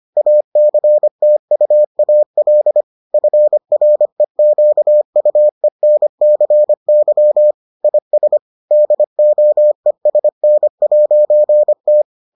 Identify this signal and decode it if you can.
Morse code, heard as a keyed tone, about 25 wpm, 600 Hz. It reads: ACTUAL FREQUENCY IS DOESN'T